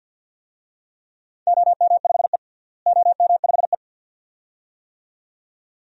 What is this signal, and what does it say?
Morse code, heard as a keyed tone, about 50 wpm, 700 Hz. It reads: OM5E OM5E
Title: DXCC-100 Call Signs - 4 Characters - Repeated 2x - 50wpm